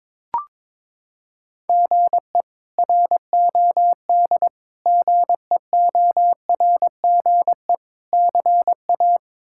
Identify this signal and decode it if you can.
Morse code, heard as a keyed tone, about 22 wpm, 700 Hz. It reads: GE ROD GEORGE CA
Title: K1USN SST QSO Practice - 22wpm